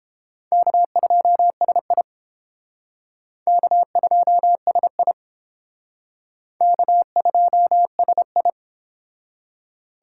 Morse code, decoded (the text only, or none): K2SI K2SI K2SI